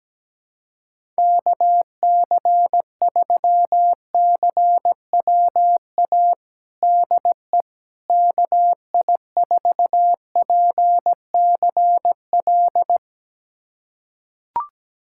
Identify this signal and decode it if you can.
Morse code, heard as a keyed tone, about 17 wpm, 700 Hz. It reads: KC3CWA DE KI4PCL
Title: USA Callsign DE Callsign 17wpm